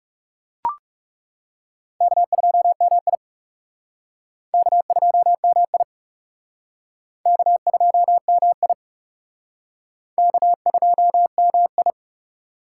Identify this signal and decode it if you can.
Morse code, heard as a keyed tone, about 37 wpm, 700 Hz. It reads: K2MI K2MI K2MI K2MI
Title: Speed Racing - USA 1x2 Call Signs 30wpm